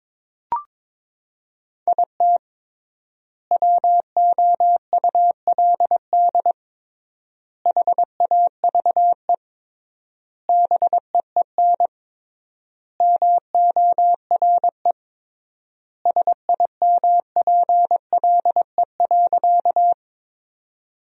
Morse code, heard as a keyed tone, about 22 wpm, 700 Hz. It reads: IT WOULD HAVE BEEN MORE SIMPLE.
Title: 3x Word-Spacing - Sentences from Top 400 Words - 22wpm